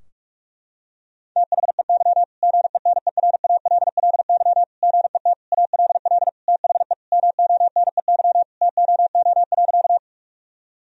Morse code, heard as a keyed tone, about 46 wpm, 700 Hz. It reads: THEY GENERALLY GET ALL THE MONEY TOO.
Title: Sentences from Top 1000 Words - Review Part 4 - 45wpm